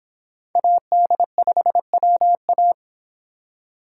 Morse code, heard as a keyed tone, about 28 wpm, 700 Hz. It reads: AD5WA